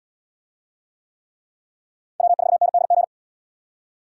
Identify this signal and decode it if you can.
Morse code, heard as a keyed tone, about 75 wpm, 700 Hz. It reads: W4NDG